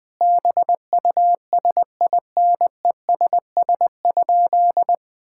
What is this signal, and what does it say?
Morse code, heard as a keyed tone, about 21 wpm, 700 Hz. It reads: BUSINESS?
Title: Sentences from Top 1000 Words - Review Part 6 - 20wpm